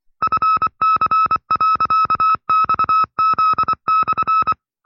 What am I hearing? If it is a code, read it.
Morse code, heard as a keyed tone, about 24 wpm, 1310 Hz. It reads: FC.XZ/
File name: FreeFiles / CW-practice / 20WPM-side2